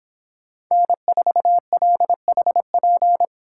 Morse code, heard as a keyed tone, about 26 wpm, 700 Hz. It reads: N4LHP